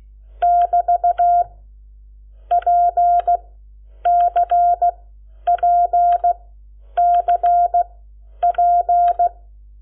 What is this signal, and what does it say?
Morse code, heard as a keyed tone, about 16 wpm, spaced as for 9 wpm, 700 Hz. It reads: <BT> PCPCP
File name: Lecciones / CLASE Nº7